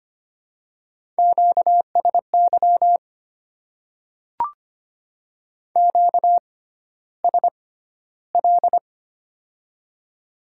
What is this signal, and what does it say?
Morse code, heard as a keyed tone, about 25 wpm, spaced as for 13 wpm, 700 Hz. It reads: QSY QSL
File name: Top 90 QSO Elements - 6x Character Spacing - 25wpm